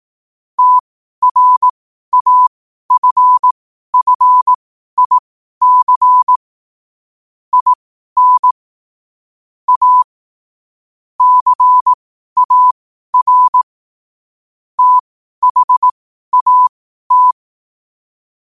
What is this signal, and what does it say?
Morse code, heard as a keyed tone, about 18 wpm, spaced as for 12 wpm, 1000 Hz. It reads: TRAFFIC IN A CAR THAT
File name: qotdfiles / 12wpm